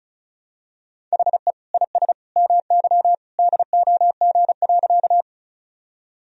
Morse code, heard as a keyed tone, about 36 wpm, 700 Hz. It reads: HE IS MY DOG.
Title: Sentences from Top 1000 Words - Review Part 2 - 35wpm